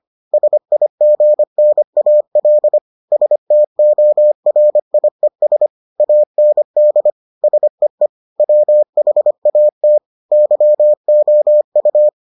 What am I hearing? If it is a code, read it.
Morse code, heard as a keyed tone, about 25 wpm, 600 Hz. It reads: SIGNAL STORIES AND SEE WHAT YOU